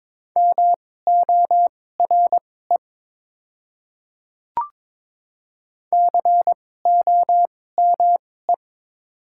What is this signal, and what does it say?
Morse code, heard as a keyed tone, about 22 wpm, spaced as for 16 wpm, 700 Hz. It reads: MORE COME
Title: 2x Character Spacing - Top 100 Words - 22wpm